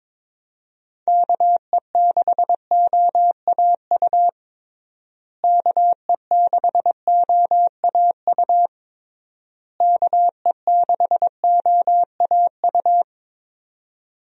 Morse code, heard as a keyed tone, about 22 wpm, 700 Hz. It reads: KE6OAU KE6OAU KE6OAU
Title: USA Call Signs - 6 Characters - Repeated 3x - 22wpm